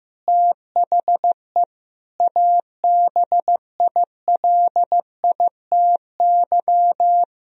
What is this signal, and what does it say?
Morse code, heard as a keyed tone, about 15 wpm, 700 Hz. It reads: THE ABILITY
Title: Sentences from Top 1000 Words - Review Part 3 - 15wpm